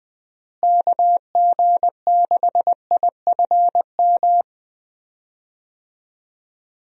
Morse code, heard as a keyed tone, about 20 wpm, 700 Hz. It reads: KG6IFM
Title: USA Call Signs - 6 Characters - Repeated 2x - 20wpm